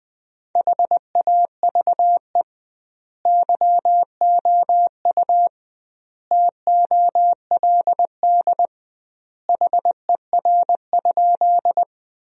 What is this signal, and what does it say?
Morse code, heard as a keyed tone, about 21 wpm, 700 Hz. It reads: HAVE YOU TOLD HER?
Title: 2x Word-Spacing - Sentences from Top 400 Words - 20wpm